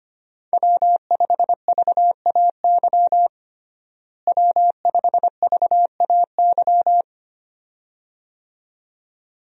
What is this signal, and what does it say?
Morse code, heard as a keyed tone, about 25 wpm, 700 Hz. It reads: W5VAY W5VAY
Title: USA Call Signs - 5 Characters - Repeated 2x - 25wpm